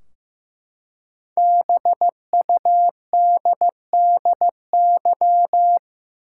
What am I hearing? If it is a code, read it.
Morse code, heard as a keyed tone, about 15 wpm, 700 Hz. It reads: BUDDY
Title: Five Letter Words - 15wpm